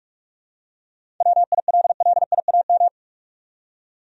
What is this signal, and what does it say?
Morse code, heard as a keyed tone, about 46 wpm, 700 Hz. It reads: WILLIAM
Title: USA Common Names - 45wpm